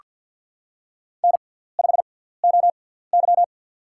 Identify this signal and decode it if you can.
Morse code, heard as a keyed tone, about 49 wpm, spaced as for 17 wpm, 700 Hz. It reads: N5OY
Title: USA Call Signs - 4 Characters - 6x Character Spacing - 50wpm